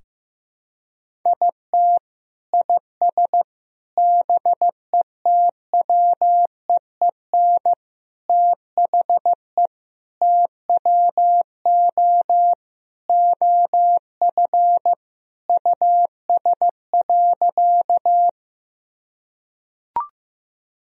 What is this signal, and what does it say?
Morse code, heard as a keyed tone, about 16 wpm, 700 Hz. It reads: IT IS BETWEEN THE TWO OF US.
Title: Sentences from Top 1000 Words - Review Part 1 - 15wpm